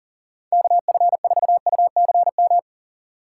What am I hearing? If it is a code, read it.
Morse code, heard as a keyed tone, about 40 wpm, 700 Hz. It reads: KF4UCM
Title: USA Call Signs - 6 Characters - 6x Character Spacing - 40wpm